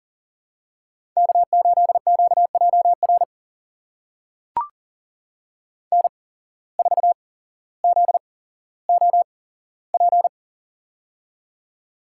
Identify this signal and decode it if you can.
Morse code, heard as a keyed tone, about 40 wpm, spaced as for 17 wpm, 700 Hz. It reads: K8QJR N4ZOP
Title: USA Call Signs - 8x Character Spacing - 40wpm